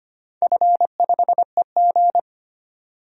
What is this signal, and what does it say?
Morse code, heard as a keyed tone, about 25 wpm, 700 Hz. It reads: F5EG
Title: DXCC-100 Call Signs - 4 Characters - 25wpm